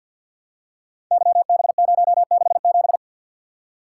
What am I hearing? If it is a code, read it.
Morse code, heard as a keyed tone, about 47 wpm, 700 Hz. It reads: YB067
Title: Code Groups - 5 Letter:Numbers - 50wpm